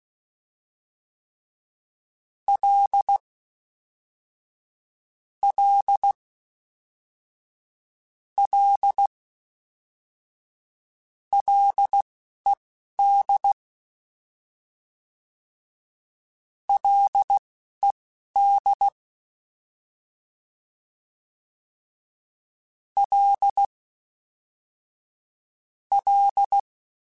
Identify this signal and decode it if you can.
Morse code, heard as a keyed tone, about 16 wpm, spaced as for 11 wpm, 800 Hz. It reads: L L L LED LED L L